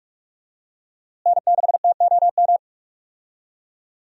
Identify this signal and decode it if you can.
Morse code, heard as a keyed tone, about 41 wpm, 700 Hz. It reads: N6TOM
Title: USA Call Signs - 5 Characters - Repeated 2x - 45wpm